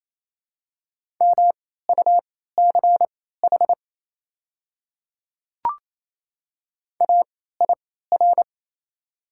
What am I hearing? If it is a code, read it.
Morse code, heard as a keyed tone, about 28 wpm, spaced as for 16 wpm, 690 Hz. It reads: MUCH AIR
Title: 3x Character Spacing - Top 200 Words - 28wpm